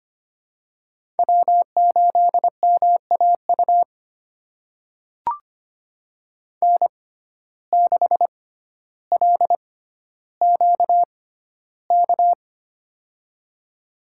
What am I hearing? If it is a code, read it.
Morse code, heard as a keyed tone, about 25 wpm, spaced as for 13 wpm, 700 Hz. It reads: W8MAU N6LQK
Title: USA Call Signs - 5 Characters - 6x Character Spacing - 25wpm